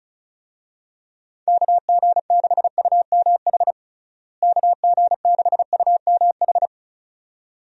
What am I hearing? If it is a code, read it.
Morse code, heard as a keyed tone, about 36 wpm, 700 Hz. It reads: KG6UMH KG6UMH
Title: USA Call Signs - 6 Characters - Repeated 2x - 35wpm